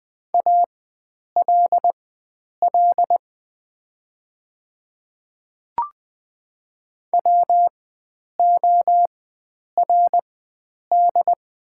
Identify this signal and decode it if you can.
Morse code, heard as a keyed tone, about 20 wpm, spaced as for 9 wpm, 700 Hz. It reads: ALL WORD
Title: 4x Character Spacing - Top 300 Words - 20wpm